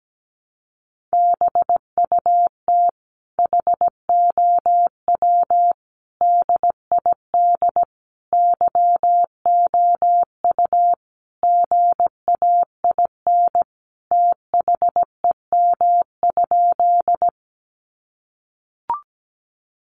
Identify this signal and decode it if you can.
Morse code, heard as a keyed tone, about 17 wpm, 700 Hz. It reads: BUT HOW DID YOU GAIN THEM?
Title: Sentences from Top 1000 Words 17wpm